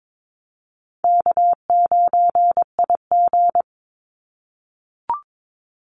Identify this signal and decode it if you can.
Morse code, heard as a keyed tone, about 22 wpm, 700 Hz. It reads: K9IG
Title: Call Signs - 22wpm